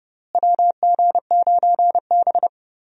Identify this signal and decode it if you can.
Morse code, heard as a keyed tone, about 31 wpm, 700 Hz. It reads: WG9B
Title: USA Call Signs - 4 Characters - Repeated 2x - 30wpm